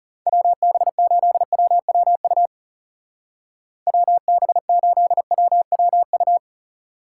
Morse code, heard as a keyed tone, about 38 wpm, 700 Hz. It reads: WB8WWU WB8WWU